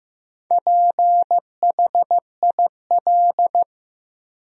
Morse code, heard as a keyed tone, about 15 wpm, 700 Hz. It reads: PHIL
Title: USA Common Names - Max 4 Letters - 15wpm